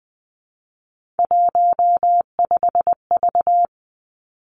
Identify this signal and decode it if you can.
Morse code, heard as a keyed tone, about 20 wpm, 700 Hz. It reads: 15V
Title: Three Challenging Letter-Number Combinations 20wpm